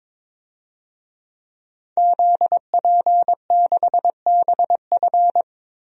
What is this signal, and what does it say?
Morse code, heard as a keyed tone, about 22 wpm, 700 Hz. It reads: ZP6BF
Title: DXCC-100 Call Signs - 5 Characters - Repeated 3x - 22wpm